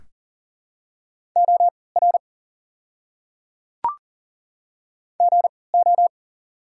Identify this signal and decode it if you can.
Morse code, heard as a keyed tone, about 40 wpm, spaced as for 23 wpm, 700 Hz. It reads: OR GO
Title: 3x Character Spacing - Top 100 Words - 40wpm